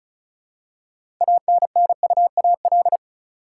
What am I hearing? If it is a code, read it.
Morse code, heard as a keyed tone, about 34 wpm, 700 Hz. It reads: ANNUAL
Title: Six Letter Words - 35wpm